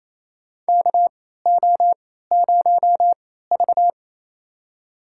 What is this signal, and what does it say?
Morse code, heard as a keyed tone, about 28 wpm, spaced as for 16 wpm, 700 Hz. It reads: KO0V
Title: USA Call Signs - 4 Characters - 3x Character Spacing - 28wpm